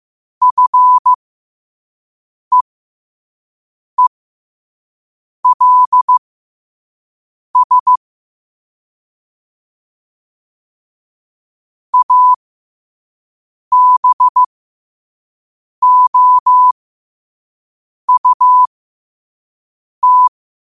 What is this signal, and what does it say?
Morse code, heard as a keyed tone, about 15 wpm, spaced as for 5 wpm, 1000 Hz. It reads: FEELS ABOUT
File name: qotdfiles / 5wpm